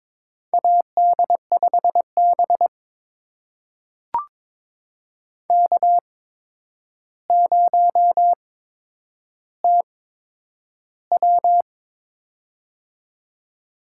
Morse code, heard as a keyed tone, about 21 wpm, spaced as for 9 wpm, 700 Hz. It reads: AD5B K0TW
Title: USA Call Signs - 4 Characters - 8x Character Spacing - 22wpm